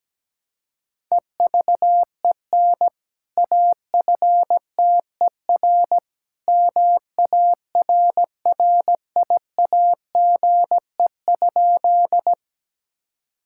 Morse code, heard as a keyed tone, about 17 wpm, 700 Hz. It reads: EVEN AFTER MARRIAGE?
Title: Sentences from Top 1000 Words - Review Part 5 - 17wpm